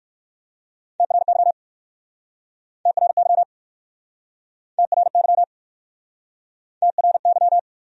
Morse code, heard as a keyed tone, about 58 wpm, 700 Hz. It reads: TRY TRY TRY TRY